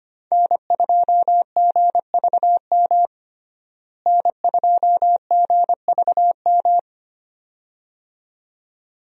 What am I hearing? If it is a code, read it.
Morse code, heard as a keyed tone, about 25 wpm, 700 Hz. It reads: N2GVM N2GVM
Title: USA Call Signs - 5 Characters - Repeated 2x - 25wpm